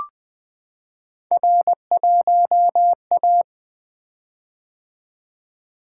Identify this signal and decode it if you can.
Morse code, heard as a keyed tone, about 20 wpm, 700 Hz. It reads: R1A
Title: Three Letter-Number Combinations - 20wpm